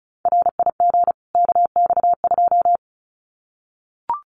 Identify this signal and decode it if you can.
Morse code, heard as a keyed tone, about 34 wpm, 700 Hz. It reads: RIG KX2
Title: Common QSO Phrases 35wpm